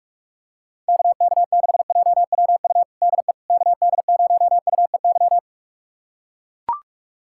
Morse code, heard as a keyed tone, about 43 wpm, 700 Hz. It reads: KK6JWU DE KD0UEY
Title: USA Callsign DE Callsign 45wpm